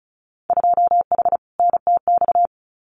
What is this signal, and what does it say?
Morse code, heard as a keyed tone, about 36 wpm, 700 Hz. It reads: 2H NTX
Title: Winter Field Day Exchanges Encore 35wpm